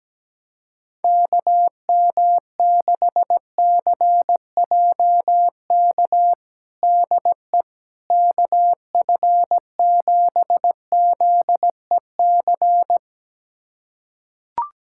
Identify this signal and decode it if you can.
Morse code, heard as a keyed tone, about 17 wpm, 700 Hz. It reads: KM6CJK DE KF7ZEC
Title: USA Callsign DE Callsign - 6 Characters - 17wpm